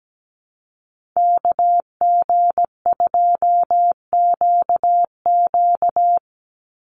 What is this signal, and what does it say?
Morse code, heard as a keyed tone, about 17 wpm, 700 Hz. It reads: KG2QQ
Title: USA Call Signs 17wpm